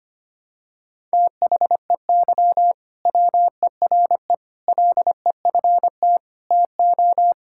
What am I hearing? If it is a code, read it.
Morse code, heard as a keyed tone, about 25 wpm, 700 Hz. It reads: THEY WERE LEFT TO
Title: Sentences from Top 1000 Words - Review Part 5 - 25wpm